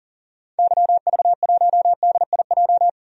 Morse code, heard as a keyed tone, about 40 wpm, 700 Hz. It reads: YV1DIJ
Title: DXCC-100 Call Signs - 6 Characters - 40wpm